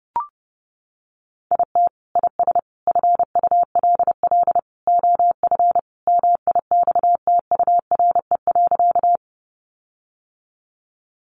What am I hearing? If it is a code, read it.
Morse code, heard as a keyed tone, about 30 wpm, 700 Hz. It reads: IT IS FULL OF MIXTURE.
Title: Sentences from Top 1500 Words 30wpm